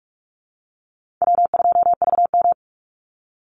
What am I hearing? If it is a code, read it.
Morse code, heard as a keyed tone, about 46 wpm, 700 Hz. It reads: W2VM
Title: Easy Call Signs - 45wpm